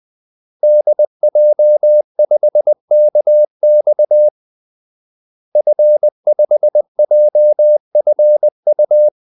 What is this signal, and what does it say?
Morse code, heard as a keyed tone, about 20 wpm, 600 Hz. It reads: DJ5KX F5JFU
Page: Kaldesignaler 20 wpm